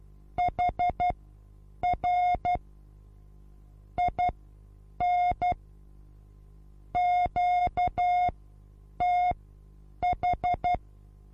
Morse code, heard as a keyed tone, about 12 wpm, spaced as for 8 wpm, 720 Hz. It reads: HR IN QTH
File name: repos / Morsekurs / DJ4UF Morsen23